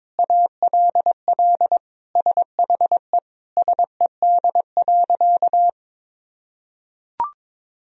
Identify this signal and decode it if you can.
Morse code, heard as a keyed tone, about 22 wpm, 700 Hz. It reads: ALL SHE SED.
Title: Sentences from Top 100 Words and Abbreviations 22wpm